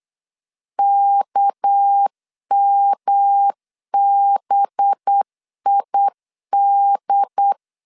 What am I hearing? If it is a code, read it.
Morse code, heard as a keyed tone, about 8 wpm, 800 Hz. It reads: KMBID
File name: LEZIONI / 8 YJ45 / 8 YJ 8wpm 3313